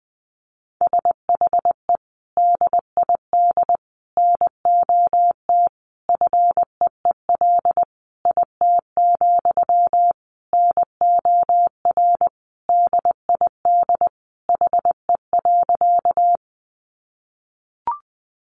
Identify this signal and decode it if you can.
Morse code, heard as a keyed tone, about 20 wpm, 700 Hz. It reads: SHE DID NOT FEEL IT, NOR DID HE.
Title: Sentences from Top 1500 Words 20wpm